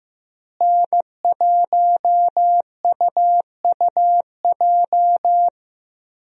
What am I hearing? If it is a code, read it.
Morse code, heard as a keyed tone, about 15 wpm, 700 Hz. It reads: N1UUJ
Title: USA Call Signs - 5 Characters - Repeated 3x - 15wpm